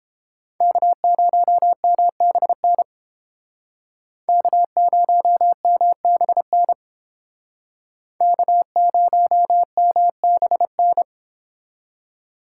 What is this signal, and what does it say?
Morse code, heard as a keyed tone, about 28 wpm, 700 Hz. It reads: K0MBN K0MBN K0MBN